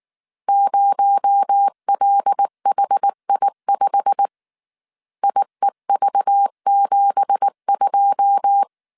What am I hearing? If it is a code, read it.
Morse code, heard as a keyed tone, about 19 wpm, 800 Hz. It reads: 0LHI5 IEV72
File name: CW / DET / 1 9 1500C S A 18wpm 2459